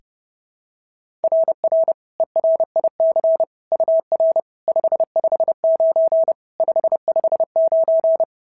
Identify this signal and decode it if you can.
Morse code, heard as a keyed tone, about 30 wpm, 690 Hz. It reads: RR ERIC UR 559 559
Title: SOTA QSO Practice 30wpm